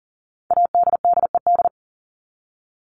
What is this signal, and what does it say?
Morse code, heard as a keyed tone, about 40 wpm, 700 Hz. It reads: ADDED
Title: ED Suffix - Top 100 Words with Base Words 40wpm